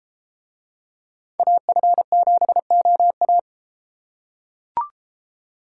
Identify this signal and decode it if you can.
Morse code, heard as a keyed tone, about 32 wpm, 700 Hz. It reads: AF7OA